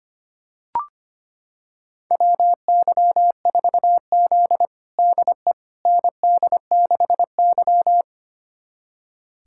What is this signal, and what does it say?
Morse code, heard as a keyed tone, about 25 wpm, 700 Hz. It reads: WY4Z DE ND6Y
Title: USA Callsign DE Callsign - 4 Characters - 25wpm